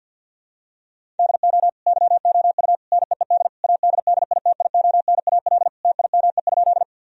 Morse code, heard as a keyed tone, about 50 wpm, 700 Hz. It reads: DO YOU NEED ADDITIONAL TIME?
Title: Sentences from Top 1000 Words - Review Part 4 - 50wpm